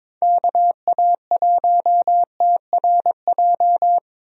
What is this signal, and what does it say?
Morse code, heard as a keyed tone, about 22 wpm, 700 Hz. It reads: KA1TRJ
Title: USA Call Signs - 6 Characters - Repeated 2x - 22wpm